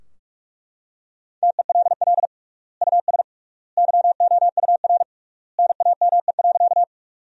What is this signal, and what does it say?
Morse code, heard as a keyed tone, about 42 wpm, 700 Hz. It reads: TELL US YOUR NAME.
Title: 3x Word-Spacing - Sentences from Top 200 Words - 45wpm